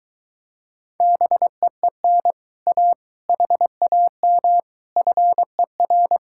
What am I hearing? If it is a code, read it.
Morse code, heard as a keyed tone, about 23 wpm, 700 Hz. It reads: BEEN A HAM FER